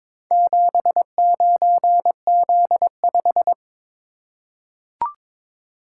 Morse code, heard as a keyed tone, about 23 wpm, 700 Hz. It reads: 79Z5
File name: practice / 22wpm / Four Letter-Number Combinations - 22wpm